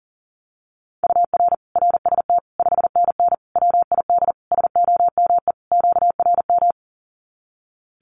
Morse code, heard as a keyed tone, about 37 wpm, 700 Hz. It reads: UR RST 5NN WID SOME QRM
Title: Common QSO Phrases 40wpm